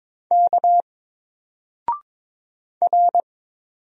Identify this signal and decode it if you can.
Morse code, heard as a keyed tone, about 22 wpm, 700 Hz. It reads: K R
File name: Single-Letter-Number-22wpm